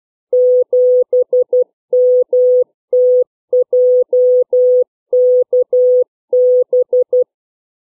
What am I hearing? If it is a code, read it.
Morse code, heard as a keyed tone, about 12 wpm, 500 Hz. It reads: 7MTJKB
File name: CW / letter number groups 3